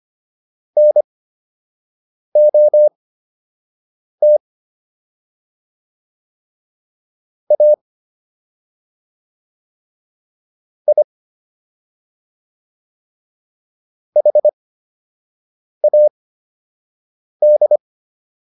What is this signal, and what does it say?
Morse code, heard as a keyed tone, about 25 wpm, spaced as for 6 wpm, 600 Hz. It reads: NOT A I HAD